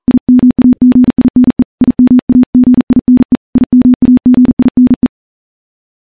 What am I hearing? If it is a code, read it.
Morse code, heard as a keyed tone, about 40 wpm, 250 Hz. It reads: IMAGINE IMAGINE IMAGINE
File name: morse-training-40WPM